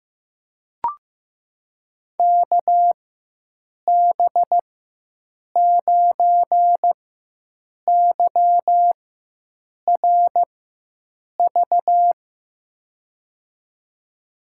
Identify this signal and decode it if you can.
Morse code, heard as a keyed tone, about 15 wpm, spaced as for 7 wpm, 700 Hz. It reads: KB9YRV
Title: USA Call Signs - 6 Characters - 4x Character Spacing - 15wpm